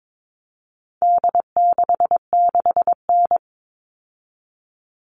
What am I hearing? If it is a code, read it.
Morse code, heard as a keyed tone, about 22 wpm, 700 Hz. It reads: D66N
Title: All Cut Numbers - 4-Digit 22wpm